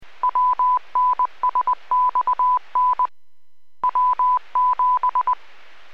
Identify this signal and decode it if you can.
Morse code, heard as a keyed tone, about 20 wpm, 1000 Hz. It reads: WNSXN W7